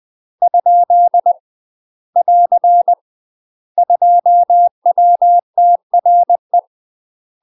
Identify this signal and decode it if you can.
Morse code, heard as a keyed tone, about 20 wpm, 700 Hz. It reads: ? <AR> 2WTRE
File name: UK MASTER QSOs TRAINING / 20WPM / UK MASTER QSOs 0010